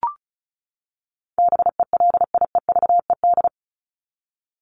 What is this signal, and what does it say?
Morse code, heard as a keyed tone, about 34 wpm, 700 Hz. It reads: BELIEVED
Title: ED Suffix - Top 100 Words with Base Words 35wpm